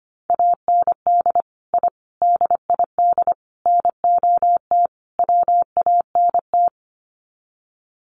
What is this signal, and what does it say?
Morse code, heard as a keyed tone, about 25 wpm, 700 Hz. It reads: AND I DID NOT WANT
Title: Sets of 5 Words 25wpm